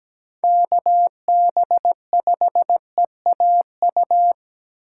Morse code, heard as a keyed tone, about 17 wpm, 700 Hz. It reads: KB5EAU
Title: USA Call Signs - 6 Characters - 3x Character Spacing - 17wpm